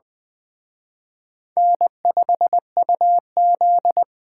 Morse code, heard as a keyed tone, about 20 wpm, 700 Hz. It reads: N5UZ
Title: Speed Racing - USA 1x2 Call Signs 17wpm